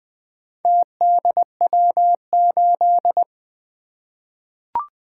Morse code, heard as a keyed tone, about 20 wpm, 700 Hz. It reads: TDW8
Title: Four Letter-Number Combinations - 20wpm